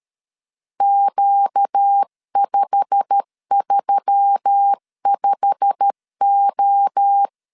Morse code, heard as a keyed tone, about 13 wpm, 800 Hz. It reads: Q535O
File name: LEZIONI / 8 YJ45 / 8 YJ45 12wpm 2151